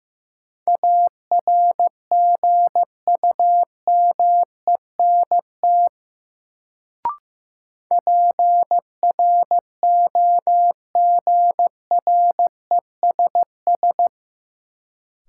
Morse code, heard as a keyed tone, about 15 wpm, 700 Hz. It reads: ARGUMENT PROGRESS
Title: Sentences from Top 1400 Words - New Word Drill Alt - 15wpm